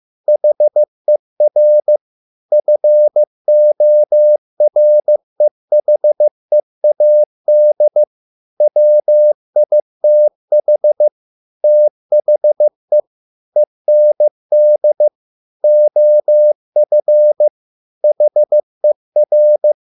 Morse code, heard as a keyed tone, about 15 wpm, 600 Hz. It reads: HER FOREHEAD WITH THE END OF HER